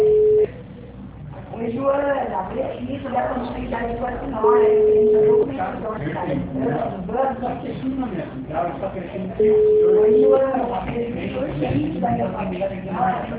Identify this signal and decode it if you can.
Ringback tone (Europe, 425 Hz), with speech.